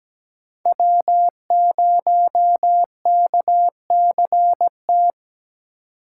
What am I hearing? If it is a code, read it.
Morse code, heard as a keyed tone, about 20 wpm, 700 Hz. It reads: W0KCT
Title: Speed Racing - USA 1x3 Call Signs 17wpm